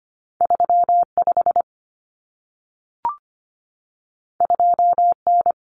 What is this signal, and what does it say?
Morse code, heard as a keyed tone, about 25 wpm, 700 Hz. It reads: 35 2N
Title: Common Cut Numbers - 2-Digit 25wpm